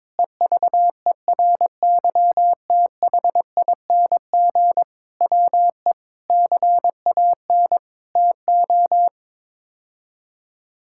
Morse code, heard as a keyed tone, about 22 wpm, 700 Hz. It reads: EVERYTHING WE CAN TO
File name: practice / 22wpm / Sets of 6 Words - 22wpm